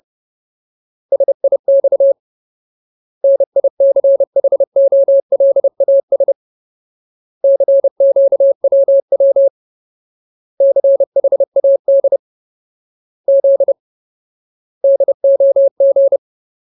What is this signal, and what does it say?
Morse code, heard as a keyed tone, about 30 wpm, 550 Hz. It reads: SIX NICHOLAS CQWW CHAD Z DOG
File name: Ultimate Word List / 30WPM Ultimate